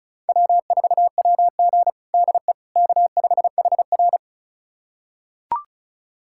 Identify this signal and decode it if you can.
Morse code, heard as a keyed tone, about 34 wpm, 700 Hz. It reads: W4WG DE K5HR
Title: USA Callsign DE Callsign - 4 Characters - 35wpm